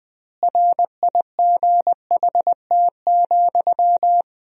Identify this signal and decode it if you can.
Morse code, heard as a keyed tone, about 20 wpm, 700 Hz. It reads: RIGHT,
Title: 3x Word-Spacing - Sentences from Top 400 Words - 20wpm